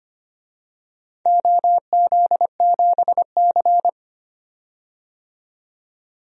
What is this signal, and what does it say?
Morse code, heard as a keyed tone, about 25 wpm, 700 Hz. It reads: OZ7C
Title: DXCC-100 Call Signs - 4 Characters - 25wpm